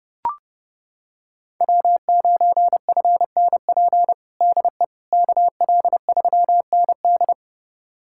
Morse code, heard as a keyed tone, about 32 wpm, 700 Hz. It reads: W9FNP DE KL3ND
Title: USA Callsign DE Callsign - 5 Characters - 30wpm